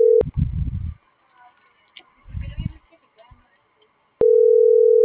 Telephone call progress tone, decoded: ringback tone

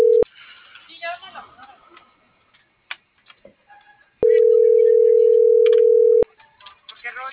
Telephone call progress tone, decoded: ringback tone

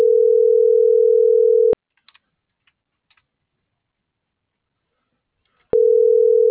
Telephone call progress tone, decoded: ringback tone

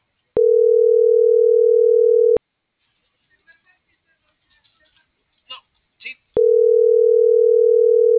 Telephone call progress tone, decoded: ringback tone

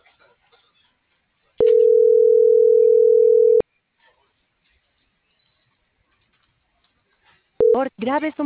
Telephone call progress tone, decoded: ringback tone

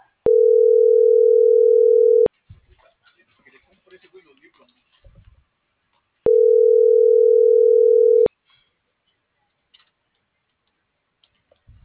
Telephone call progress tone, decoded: ringback tone

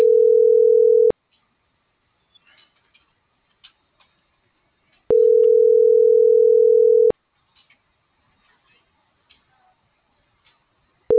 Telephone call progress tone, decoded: ringback tone

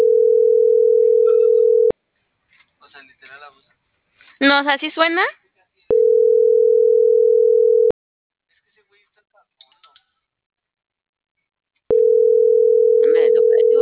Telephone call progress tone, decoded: ringback tone

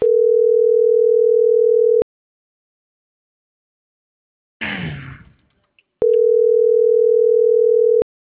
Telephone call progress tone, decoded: ringback tone

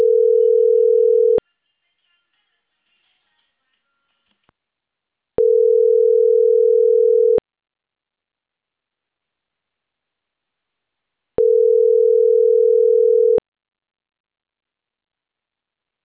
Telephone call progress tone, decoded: ringback tone

